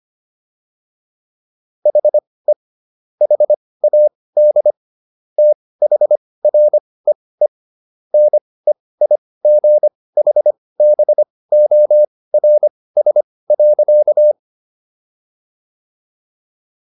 Morse code, heard as a keyed tone, about 25 wpm, spaced as for 18 wpm, 600 Hz. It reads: HE HAD THREE NEIGHBORS.